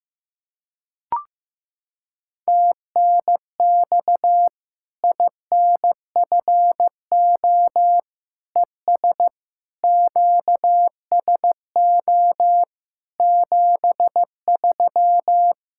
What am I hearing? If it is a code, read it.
Morse code, heard as a keyed tone, about 16 wpm, 700 Hz. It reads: TNX INFO ES QSO 73
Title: Practice QSO - Exchange 4 - Ending - 15wpm